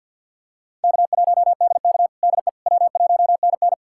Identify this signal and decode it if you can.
Morse code, heard as a keyed tone, about 52 wpm, 700 Hz. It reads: K1DK DE W1NN
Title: USA Callsign DE Callsign - 4 Characters - 50wpm